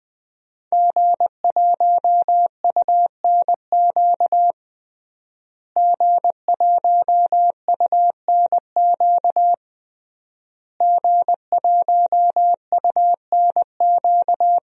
Morse code, heard as a keyed tone, about 20 wpm, 700 Hz. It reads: G1UNQ G1UNQ G1UNQ
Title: DXCC-100 Call Signs - 5 Characters - Repeated 3x - 20wpm